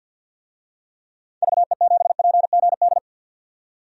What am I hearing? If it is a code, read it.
Morse code, heard as a keyed tone, about 53 wpm, 700 Hz. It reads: VE7PGD
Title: DXCC-100 Call Signs - 6 Characters - 50wpm